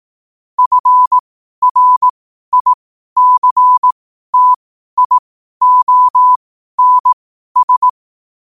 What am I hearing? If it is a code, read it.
Morse code, heard as a keyed tone, about 18 wpm, spaced as for 12 wpm, 1000 Hz. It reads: FRICTIONS
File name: qotdfiles / 12wpm